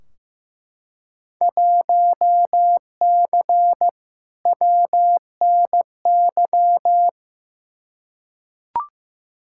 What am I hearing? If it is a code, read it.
Morse code, heard as a keyed tone, about 15 wpm, 700 Hz. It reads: 1C WNY
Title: Field Day Exchanges Encore - 15wpm